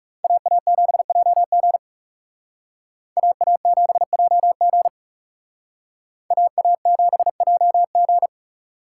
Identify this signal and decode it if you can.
Morse code, heard as a keyed tone, about 36 wpm, 700 Hz. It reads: AA7JG AA7JG AA7JG
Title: Speed Racing - USA 2x2 Call Signs 30wpm